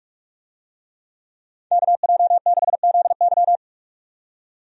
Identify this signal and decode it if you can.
Morse code, heard as a keyed tone, about 41 wpm, 700 Hz. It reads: KJ6ZY
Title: USA Call Signs - 5 Characters - 45wpm